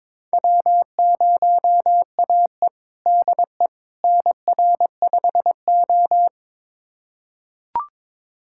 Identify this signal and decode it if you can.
Morse code, heard as a keyed tone, about 21 wpm, 700 Hz. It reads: W0AE DE NR5O
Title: USA Callsign DE Callsign - 4 Characters - 22wpm